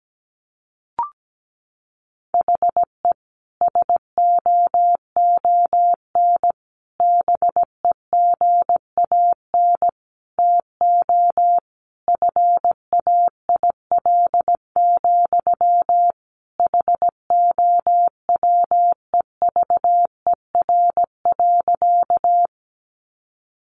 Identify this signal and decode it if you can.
Morse code, heard as a keyed tone, about 17 wpm, 700 Hz. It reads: HE SOON BEGAN TO FAIL, HOWEVER.
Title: Sentences from Top 1100 Words 17wpm